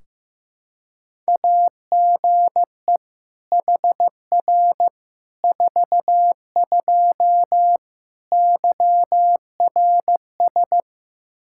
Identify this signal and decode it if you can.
Morse code, heard as a keyed tone, about 16 wpm, 700 Hz. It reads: AGE HR 42 YRS
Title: Practice QSO - Exchange 3 - More Info - 15wpm